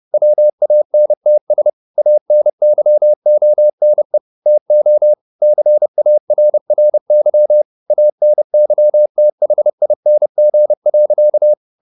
Morse code, heard as a keyed tone, about 30 wpm, 600 Hz. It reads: WANTS ANYONE TO CARRY ANYTHING.
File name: the old man and the sea / the old man and the sea 1 30wpm